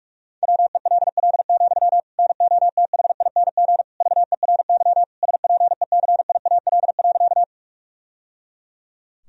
Morse code, heard as a keyed tone, about 43 wpm, 700 Hz. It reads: WELL, NOTHING VERY SPECIAL.
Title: Sentences from Top 1000 Words - Review Part 3 - 45wpm